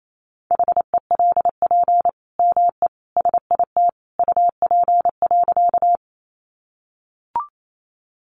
Morse code, heard as a keyed tone, about 28 wpm, 700 Hz. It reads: HELP ME SIT UP.
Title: Sentences from Top 500 Words 28wpm